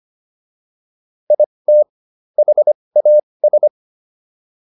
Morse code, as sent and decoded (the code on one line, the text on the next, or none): .. - / .... .- ...
IT HAS